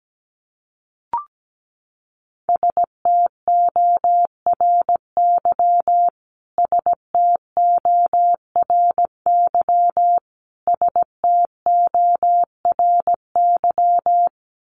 Morse code, heard as a keyed tone, about 17 wpm, 700 Hz. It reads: STORY STORY STORY
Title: Top 300 Words Repeated 17wpm